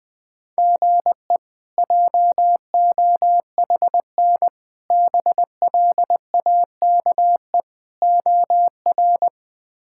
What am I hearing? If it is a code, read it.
Morse code, heard as a keyed tone, about 21 wpm, 700 Hz. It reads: GE JOHN BLAKE OR
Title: K1USN SST QSO Practice - 20wpm